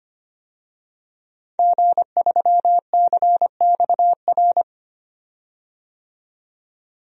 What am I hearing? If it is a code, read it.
Morse code, heard as a keyed tone, about 25 wpm, 690 Hz. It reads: G3CXR
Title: DXCC-100 Call Signs - 5 Characters - 25wpm